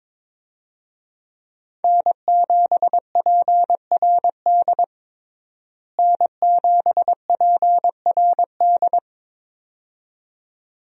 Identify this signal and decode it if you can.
Morse code, heard as a keyed tone, about 21 wpm, 700 Hz. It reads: N7PRD N7PRD
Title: USA Call Signs - 5 Characters - Repeated 2x - 22wpm